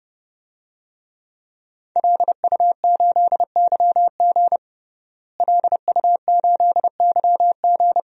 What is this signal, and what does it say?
Morse code, heard as a keyed tone, about 30 wpm, 700 Hz. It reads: LU8YG LU8YG
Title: DXCC-100 Call Signs - 5 Characters - Repeated 3x - 30wpm